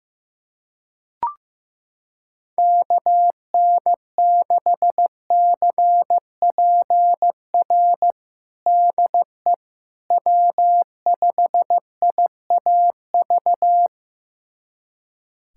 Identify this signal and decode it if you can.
Morse code, heard as a keyed tone, about 15 wpm, 700 Hz. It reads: KN6CPR DE W5IAV
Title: USA Callsign DE Callsign 15wpm